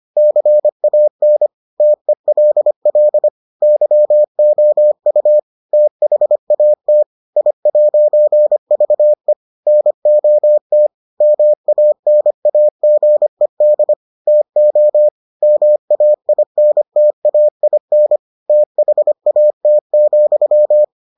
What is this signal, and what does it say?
Morse code, heard as a keyed tone, about 25 wpm, 600 Hz. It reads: CAN TELL YOU THAT I'VE NOT MANAGED TO MAINTAIN THAT,